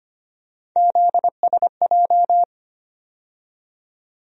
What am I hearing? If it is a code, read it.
Morse code, heard as a keyed tone, about 25 wpm, 700 Hz. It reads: ZSJ
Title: USA Call Sign Postfixes - 3 Letters 25wpm